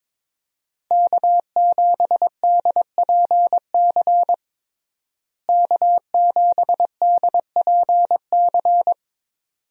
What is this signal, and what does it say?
Morse code, heard as a keyed tone, about 22 wpm, 700 Hz. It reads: K7DPC K7DPC
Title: USA Call Signs - 5 Characters - Repeated 2x - 22wpm